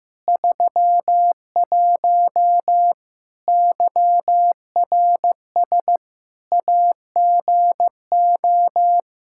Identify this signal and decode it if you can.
Morse code, heard as a keyed tone, about 15 wpm, 700 Hz. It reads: 31 YRS AGO
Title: Practice QSO - Exchange 3 - More Info - 15wpm